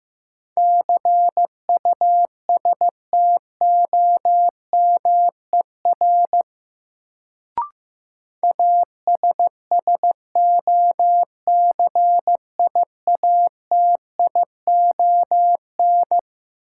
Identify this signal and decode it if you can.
Morse code, heard as a keyed tone, about 16 wpm, 700 Hz. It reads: CUSTOMER ASSOCIATION
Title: Sentences from Top 1100 Words - New Word Drill - 15wpm